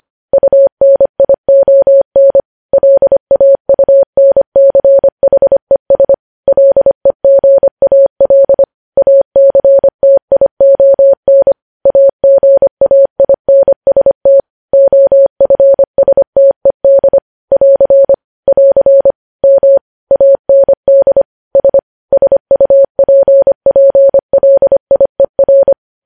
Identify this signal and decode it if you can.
Morse code, heard as a keyed tone, about 25 wpm, 550 Hz. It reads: UNION LAUNCHES LEGAL ACTION AGAINST OFSTED <AR> <AR> M AND S SUPPLIER
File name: podcast / News headlines-1x-25x25